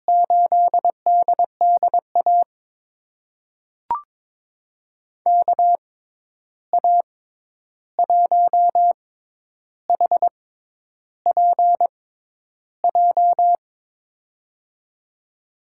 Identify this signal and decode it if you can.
Morse code, heard as a keyed tone, about 22 wpm, spaced as for 10 wpm, 700 Hz. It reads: 8DDA KA1HPJ
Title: USA Call Signs - 6 Characters - 6x Character Spacing - 22wpm